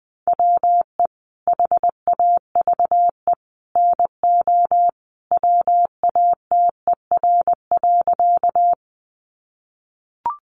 Morse code, heard as a keyed tone, about 19 wpm, 700 Hz. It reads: WE HAVE NO WATER.
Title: Sentences from Top 100 Words 20wpm